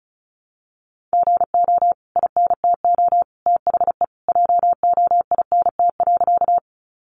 Morse code, heard as a keyed tone, about 34 wpm, 700 Hz. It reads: GO INTO THE JOINT.
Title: Sentences from Top 1300 Words 35wpm